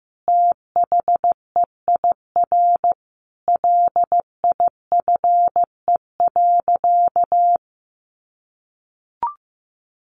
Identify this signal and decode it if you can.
Morse code, heard as a keyed tone, about 15 wpm, 700 Hz. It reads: THEIR LIFE.
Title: Sentences from Top 1200 Words 15wpm